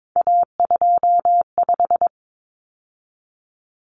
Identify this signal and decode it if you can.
Morse code, heard as a keyed tone, about 22 wpm, 700 Hz. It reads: A25
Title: Common Cut Numbers - 3-Digit 22wpm